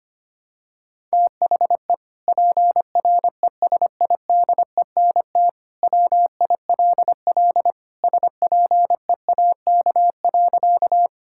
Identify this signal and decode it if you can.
Morse code, heard as a keyed tone, about 25 wpm, 700 Hz. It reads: THE PRESIDENT WILL SPEAK.
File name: practice / 25wpm / Sentences with 2-Word Phrases - 25wpm